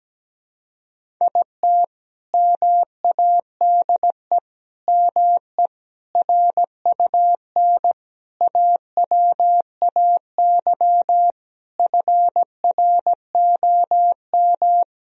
Morse code, heard as a keyed tone, about 17 wpm, 700 Hz. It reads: IT MADE ME RUN AWAY FROM
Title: Sentences from Top 1000 Words - Review Part 1 - 17wpm